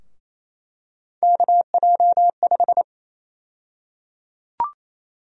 Morse code, heard as a keyed tone, about 28 wpm, 700 Hz. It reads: KJ5
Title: USA Call Sign Prefixes - 28wpm